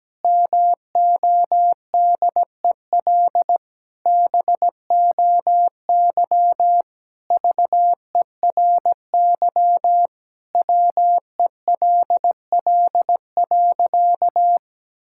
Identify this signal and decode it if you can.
Morse code, heard as a keyed tone, about 17 wpm, 700 Hz. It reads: MODEL BOY VERY WELL.
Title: Sentences from Top 1000 Words - Review Part 5 - 17wpm